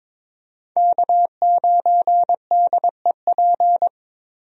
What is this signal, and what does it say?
Morse code, heard as a keyed tone, about 22 wpm, 700 Hz. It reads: K9DEP
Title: USA Call Signs - 5 Characters - Repeated 3x - 22wpm